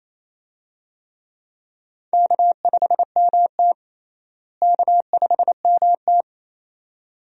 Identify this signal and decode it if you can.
Morse code, heard as a keyed tone, about 28 wpm, 700 Hz. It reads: K5MT K5MT
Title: USA Call Signs - 4 Characters - Repeated 2x - 28wpm